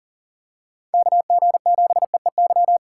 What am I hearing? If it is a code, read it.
Morse code, heard as a keyed tone, about 40 wpm, 700 Hz. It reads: KG7EEY
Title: USA Call Signs - 6 Characters - 6x Character Spacing - 40wpm